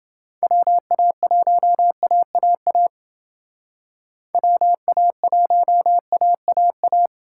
Morse code, heard as a keyed tone, about 28 wpm, 700 Hz. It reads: WA1AAA WA1AAA